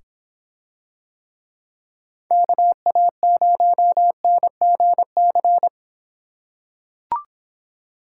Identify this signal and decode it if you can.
Morse code, heard as a keyed tone, about 26 wpm, 700 Hz. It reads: KA0NGC